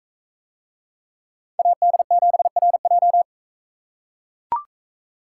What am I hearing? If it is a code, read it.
Morse code, heard as a keyed tone, about 44 wpm, 700 Hz. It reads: AD7RJ